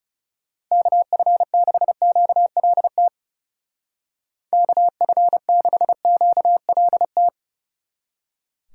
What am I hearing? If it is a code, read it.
Morse code, heard as a keyed tone, about 39 wpm, 700 Hz. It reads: KF6QLT KF6QLT